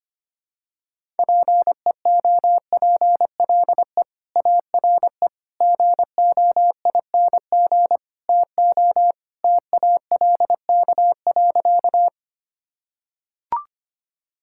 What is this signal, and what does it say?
Morse code, heard as a keyed tone, about 25 wpm, 700 Hz. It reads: PEOPLE ARE GOING TO TALK.
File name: practice / 25wpm / Sentences with 4-Word Phrases - 25wpm